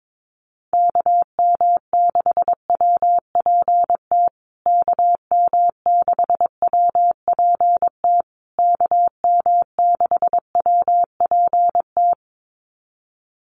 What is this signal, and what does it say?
Morse code, heard as a keyed tone, about 22 wpm, 700 Hz. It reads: KM6WPT KM6WPT KM6WPT
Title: USA Call Signs Repeated 3x 22wpm